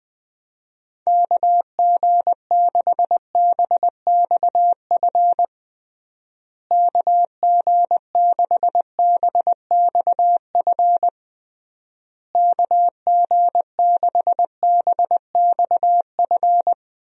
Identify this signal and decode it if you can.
Morse code, heard as a keyed tone, about 21 wpm, 700 Hz. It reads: KG6BXF KG6BXF KG6BXF
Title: USA Call Signs - 6 Characters - Repeated 3x - 20wpm